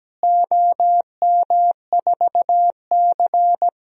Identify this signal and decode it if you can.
Morse code, heard as a keyed tone, about 17 wpm, 700 Hz. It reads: OM4C
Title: DXCC-100 Call Signs - 4 Characters - Repeated 3x - 17wpm